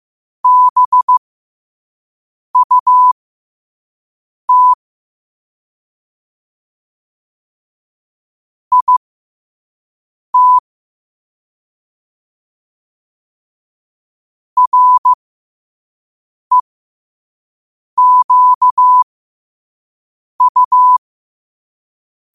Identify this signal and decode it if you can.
Morse code, heard as a keyed tone, about 15 wpm, spaced as for 5 wpm, 1000 Hz. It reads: BUT IT REQU
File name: qotdfiles / 5wpm